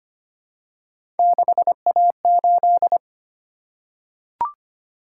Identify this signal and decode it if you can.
Morse code, heard as a keyed tone, about 25 wpm, 700 Hz. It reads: 6A8
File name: Three Letter-Number Combinations - 25wpm